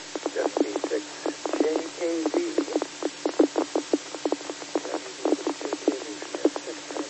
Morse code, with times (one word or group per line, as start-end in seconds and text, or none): none